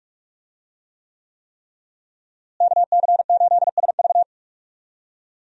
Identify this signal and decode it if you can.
Morse code, heard as a keyed tone, about 45 wpm, 700 Hz. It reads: KC8SV
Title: USA Call Signs - 5 Characters - Repeated 3x - 45wpm